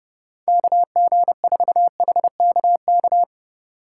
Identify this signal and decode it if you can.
Morse code, heard as a keyed tone, about 38 wpm, 700 Hz. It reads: KG4HKK
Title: Speed Racing - USA 2x3 Call Signs 30wpm